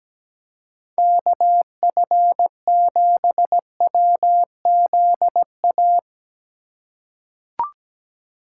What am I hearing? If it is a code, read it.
Morse code, heard as a keyed tone, about 17 wpm, 700 Hz. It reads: KF7WZA
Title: USA Call Signs - 6 Characters - 2x Character Spacing - 17wpm